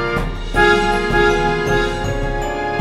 There is no code.